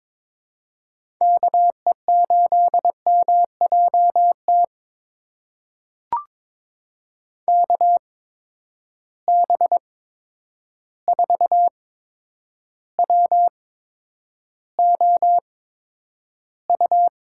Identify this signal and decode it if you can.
Morse code, heard as a keyed tone, about 22 wpm, spaced as for 9 wpm, 700 Hz. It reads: KE8MJT KB4WOU
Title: USA Call Signs - 6 Characters - 8x Character Spacing - 22wpm